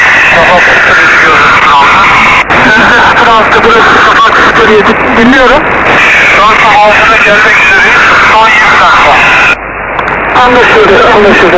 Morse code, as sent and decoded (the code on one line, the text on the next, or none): none